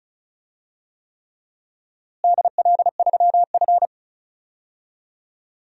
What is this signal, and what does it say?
Morse code, heard as a keyed tone, about 36 wpm, 700 Hz. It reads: DL3F
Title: DXCC-100 Call Signs - 4 Characters - Repeated 3x - 35wpm